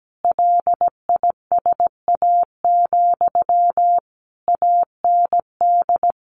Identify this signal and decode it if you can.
Morse code, heard as a keyed tone, about 17 wpm, 700 Hz. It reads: LISA, AND
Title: Additional Easy Phrases 17wpm